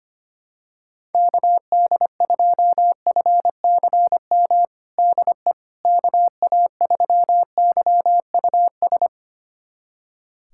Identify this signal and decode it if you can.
Morse code, heard as a keyed tone, about 25 wpm, 700 Hz. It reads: KD2FCM DE KA3YUS
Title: USA Callsign DE Callsign - 6 Characters - 25wpm